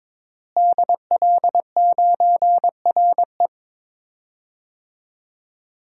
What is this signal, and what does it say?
Morse code, heard as a keyed tone, about 22 wpm, 700 Hz. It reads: DL9RE
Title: DXCC-100 Call Signs - 5 Characters - Repeated 2x - 22wpm